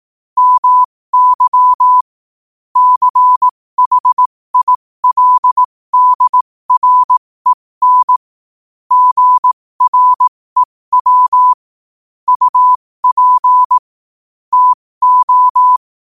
Morse code, heard as a keyed tone, about 18 wpm, spaced as for 15 wpm, 1000 Hz. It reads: MY CHILDREN GREW UP TO
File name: qotdfiles / 15wpm